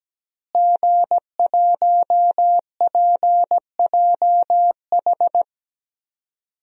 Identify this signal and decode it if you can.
Morse code, heard as a keyed tone, about 17 wpm, 700 Hz. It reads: G1PJH
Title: DXCC-100 Call Signs - 5 Characters - Repeated 3x - 17wpm